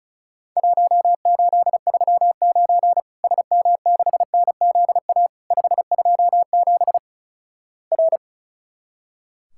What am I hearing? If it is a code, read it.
Morse code, heard as a keyed tone, about 36 wpm, 700 Hz. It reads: 1839 SM6NZA 527 R
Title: Worked All Europe DX Contest - QTC Exchange - 35wpm